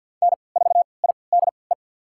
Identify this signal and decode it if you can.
Morse code, heard as a keyed tone, about 53 wpm, spaced as for 29 wpm, 700 Hz. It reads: N4IDE
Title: USA Call Signs - 5 Characters - 3x Character Spacing - 50wpm